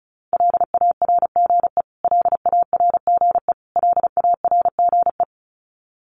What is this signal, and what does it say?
Morse code, heard as a keyed tone, about 34 wpm, 700 Hz. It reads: LARGE LARGE LARGE
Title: Top 500 Words Repeated 35wpm